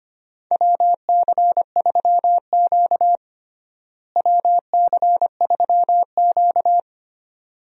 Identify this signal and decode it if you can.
Morse code, heard as a keyed tone, about 25 wpm, 700 Hz. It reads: WC3Q WC3Q
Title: USA Call Signs - 4 Characters - Repeated 2x - 25wpm